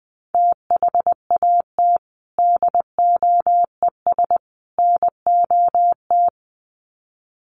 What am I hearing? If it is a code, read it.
Morse code, heard as a keyed tone, about 20 wpm, 700 Hz. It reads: THAT DOES NOT
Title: Sets of 3 Words 20wpm